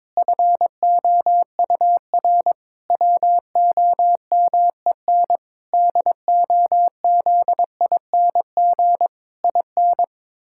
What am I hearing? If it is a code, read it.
Morse code, heard as a keyed tone, about 22 wpm, 700 Hz. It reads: FOUR WOMEN DOZING IN
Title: Pangram Challenge v2 - 22wpm